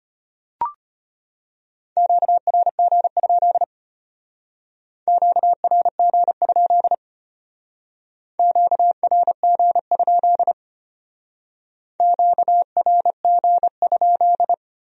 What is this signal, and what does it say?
Morse code, heard as a keyed tone, about 31 wpm, 700 Hz. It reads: QRG? QRG? QRG? QRG?
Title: Speed-Racing Q-Codes - 25wpm